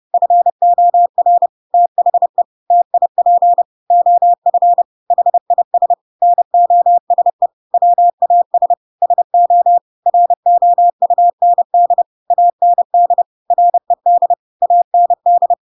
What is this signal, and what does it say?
Morse code, heard as a keyed tone, about 30 wpm, 700 Hz. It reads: FOR THE TIP OF HIS NOSE WAS SO ROUND AND RED AND
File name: Pinocchio / Pinocchio ch1 30 30 ik1hge